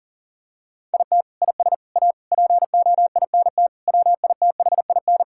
Morse code, heard as a keyed tone, about 40 wpm, 700 Hz. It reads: IT IS A POINT WITHIN
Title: Sentences from Top 1000 Words - Review Part 5 - 40wpm